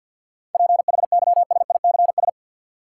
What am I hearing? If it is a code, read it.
Morse code, heard as a keyed tone, about 52 wpm, 700 Hz. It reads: PHYSICS
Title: Sentences from Top 1500 Words - New Word Drill Alt - 50wpm